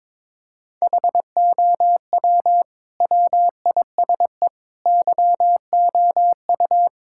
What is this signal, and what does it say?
Morse code, heard as a keyed tone, about 22 wpm, 700 Hz. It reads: HOW WISE YOU
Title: Sentences from Top 1600 Words 22wpm